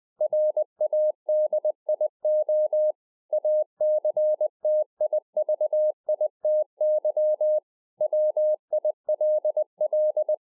Morse code, heard as a keyed tone, about 20 wpm, 610 Hz. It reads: RADIO ACTIVITY WILL
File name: 20WPM